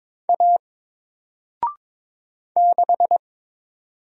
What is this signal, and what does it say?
Morse code, heard as a keyed tone, about 22 wpm, 700 Hz. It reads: A 6
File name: Single-Letter-Number-22wpm